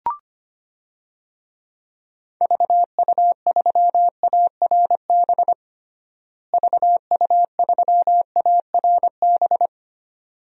Morse code, heard as a keyed tone, about 25 wpm, 700 Hz. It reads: VU3ARB VU3ARB
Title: DXCC-100 Call Signs - 6 Characters - Repeated 2x - 25wpm